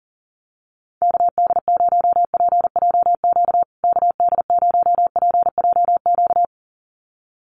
Morse code, heard as a keyed tone, about 40 wpm, 700 Hz. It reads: KD0PJQ KD0PJQ
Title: USA Call Signs Repeated 2x 40wpm